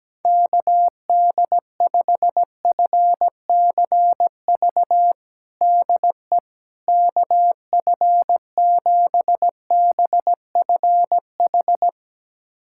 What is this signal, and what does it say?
Morse code, heard as a keyed tone, about 17 wpm, 700 Hz. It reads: KD5FCV DE KF7BFH
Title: USA Callsign DE Callsign 17wpm